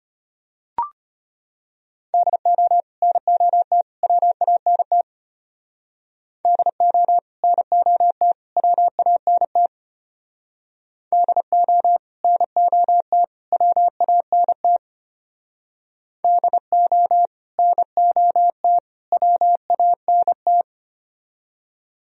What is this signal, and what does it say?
Morse code, heard as a keyed tone, about 30 wpm, 700 Hz. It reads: DO NOT WANT DO NOT WANT DO NOT WANT DO NOT WANT
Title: Speed-Racing - Sets of 3 Words 25wpm